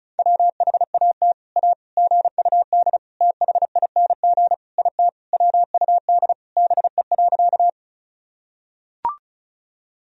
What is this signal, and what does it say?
Morse code, heard as a keyed tone, about 34 wpm, 700 Hz. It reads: WHAT A GUD THING IT WUD BE.
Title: Sentences from Top 200 Words and Abbreviations 35wpm